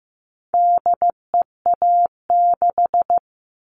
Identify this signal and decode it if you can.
Morse code, heard as a keyed tone, about 15 wpm, 700 Hz. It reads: DEA6
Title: All Cut Numbers - 4-Digit 15wpm